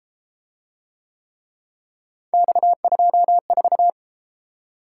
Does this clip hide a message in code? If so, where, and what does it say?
Yes: Morse code from 2.3 to 3.9 s, heard as a keyed tone, about 32 wpm, 700 Hz. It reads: X24